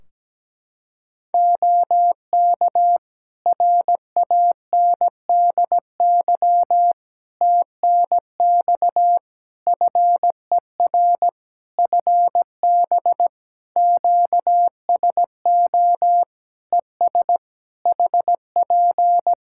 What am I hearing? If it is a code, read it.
Morse code, heard as a keyed tone, about 17 wpm, 700 Hz. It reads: OK RANDY TNX FER FB QSO ES HP
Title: Practice QSO - Exchange 4 - Ending - 17wpm